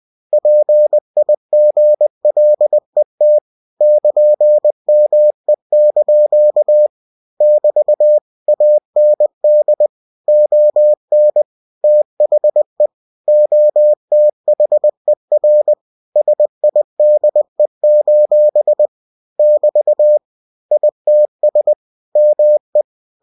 Morse code, heard as a keyed tone, about 20 wpm, 600 Hz. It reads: PIGLET <KN>ME) <BT> AND ON THE OTHER SIDE: <BT> ITS ME
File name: cw-books / winnie-the-pooh / winnie-20wpm-page-0092